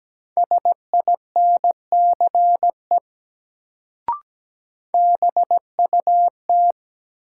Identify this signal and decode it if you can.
Morse code, heard as a keyed tone, about 17 wpm, 700 Hz. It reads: SINCE BUT
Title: Sets of 2 Words - New Word Drill - 17wpm